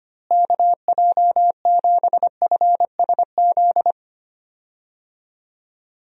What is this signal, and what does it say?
Morse code, heard as a keyed tone, about 25 wpm, 700 Hz. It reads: KJ7FSZ
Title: USA Call Signs - 6 Characters - Repeated 2x - 25wpm